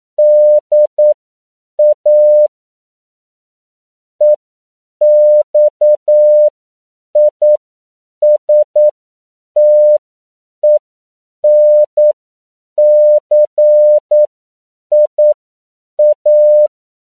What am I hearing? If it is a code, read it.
Morse code, heard as a keyed tone, about 9 wpm, spaced as for 7 wpm, 600 Hz. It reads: DA EXISTENCIA
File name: CW / 10PPM